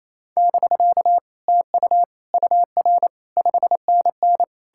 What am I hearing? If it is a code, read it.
Morse code, heard as a keyed tone, about 37 wpm, 700 Hz. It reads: <BK> TU UR 5NN